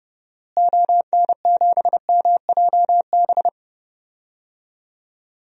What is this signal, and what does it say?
Morse code, heard as a keyed tone, about 30 wpm, 700 Hz. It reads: ON7MJB
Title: DXCC-100 Call Signs - 6 Characters - Repeated 3x - 30wpm